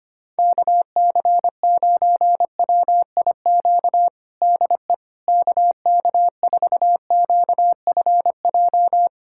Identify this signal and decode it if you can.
Morse code, heard as a keyed tone, about 25 wpm, 700 Hz. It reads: KC9WIQ DE KK4QFJ
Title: USA Callsign DE Callsign - 6 Characters - 25wpm